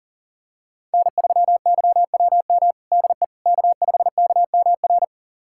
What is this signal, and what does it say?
Morse code, heard as a keyed tone, about 40 wpm, 700 Hz. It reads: N3YWM DE K5KMR
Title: USA Callsign DE Callsign - 5 Characters - 40wpm